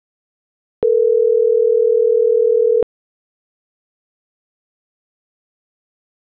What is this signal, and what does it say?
Ringback tone (North America), 440 + 480 Hz.